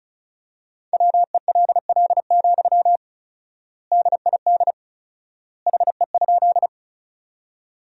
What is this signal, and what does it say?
Morse code, heard as a keyed tone, about 36 wpm, 700 Hz. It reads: WELL, DID HE?
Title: 4x Word-Spacing - Sentences from Top 200 Words - 35wpm